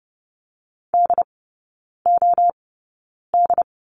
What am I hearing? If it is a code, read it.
Morse code, heard as a keyed tone, about 30 wpm, spaced as for 9 wpm, 700 Hz. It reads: DOD
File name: course / 30:10wpm / 087 Lesson 15e - Number 5 - Three Characters 30wpm